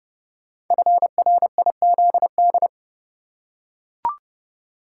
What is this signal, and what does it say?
Morse code, heard as a keyed tone, about 30 wpm, 700 Hz. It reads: FRIZD
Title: Code Groups - 5 Letter:Numbers - 30wpm